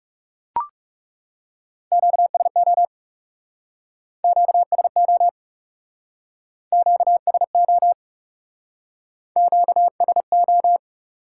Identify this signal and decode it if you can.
Morse code, heard as a keyed tone, about 39 wpm, 700 Hz. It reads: QSO QSO QSO QSO